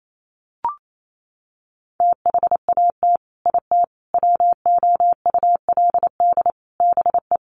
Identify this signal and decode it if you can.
Morse code, heard as a keyed tone, about 28 wpm, 700 Hz. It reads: THAT IT WOULD BE
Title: Sets of 4 Words 28wpm